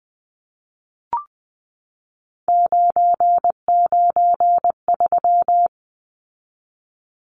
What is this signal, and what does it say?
Morse code, heard as a keyed tone, about 20 wpm, 700 Hz. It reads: 993
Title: Numbers - 3-Digit 20wpm